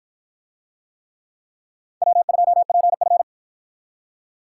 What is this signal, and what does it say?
Morse code, heard as a keyed tone, about 48 wpm, 700 Hz. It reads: W2PF